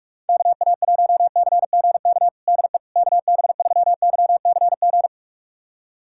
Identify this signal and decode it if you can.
Morse code, heard as a keyed tone, about 42 wpm, 700 Hz. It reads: KA1CGK DE KB3YCG
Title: USA Callsign DE Callsign - 6 Characters - 45wpm